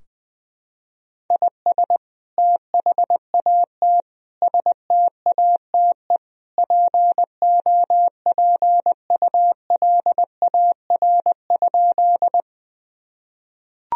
Morse code, heard as a keyed tone, about 20 wpm, 700 Hz. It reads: IS THAT STATE POPULAR?
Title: Sentences from Top 1000 Words - Review Part 3 - 20wpm